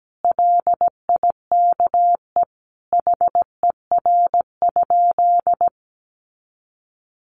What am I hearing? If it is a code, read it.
Morse code, heard as a keyed tone, about 17 wpm, 700 Hz. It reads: LIKE HER?
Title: Sentences from Top 100 Words 17wpm